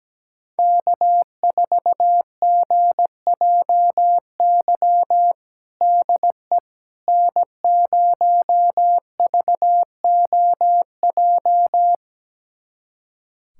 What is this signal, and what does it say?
Morse code, heard as a keyed tone, about 17 wpm, 700 Hz. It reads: K4GJY DE N0VOJ
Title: USA Callsign DE Callsign - 5 Characters - 17wpm